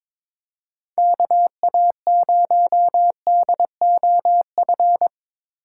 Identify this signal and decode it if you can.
Morse code, heard as a keyed tone, about 22 wpm, 700 Hz. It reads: KA0DOF
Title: USA Call Signs - 3x Character Spacing - 22wpm